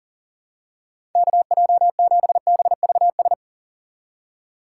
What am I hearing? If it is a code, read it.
Morse code, heard as a keyed tone, about 40 wpm, 700 Hz. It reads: KJ7BVS
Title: USA Call Signs - 6 Characters - 6x Character Spacing - 40wpm